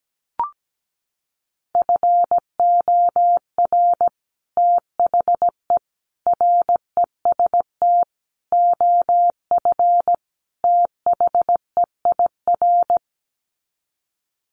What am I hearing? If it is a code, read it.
Morse code, heard as a keyed tone, about 17 wpm, 700 Hz. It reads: FOR THE REST OF THEIR
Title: Sets of 5 Words Encore2 17wpm